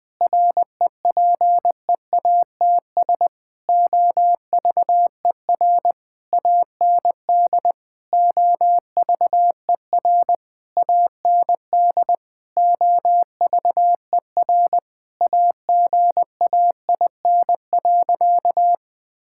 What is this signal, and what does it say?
Morse code, heard as a keyed tone, about 19 wpm, 700 Hz. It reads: REPEATS OVER AND OVER AND OVER AGAIN.
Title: Sentences with 6-Word Phrases - 20wpm